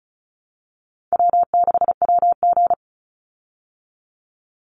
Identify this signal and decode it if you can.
Morse code, heard as a keyed tone, about 34 wpm, 700 Hz. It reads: W6WG
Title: Call Signs Repeated 3x - 35wpm